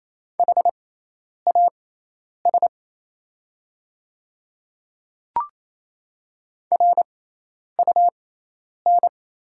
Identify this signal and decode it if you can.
Morse code, heard as a keyed tone, about 28 wpm, spaced as for 10 wpm, 700 Hz. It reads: HAS RUN
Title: 6x Character Spacing - Top 300 Words - 28wpm